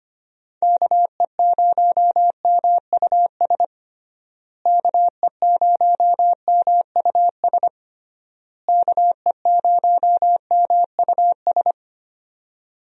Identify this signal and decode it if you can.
Morse code, heard as a keyed tone, about 25 wpm, 700 Hz. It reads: KE0MUS KE0MUS KE0MUS
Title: USA Call Signs - 6 Characters - Repeated 3x - 25wpm